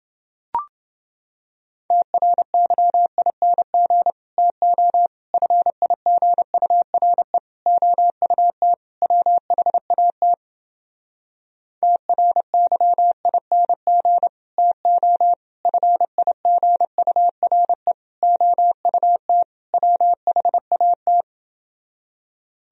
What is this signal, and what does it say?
Morse code, heard as a keyed tone, about 29 wpm, 700 Hz. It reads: TRYING TO FIGURE OUT WHAT TRYING TO FIGURE OUT WHAT